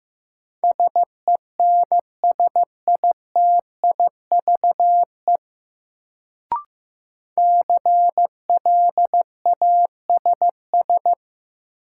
Morse code, heard as a keyed tone, about 15 wpm, 700 Hz. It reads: SENSITIVE CLASS
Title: Sentences from Top 1400 Words - New Word Drill Alt - 15wpm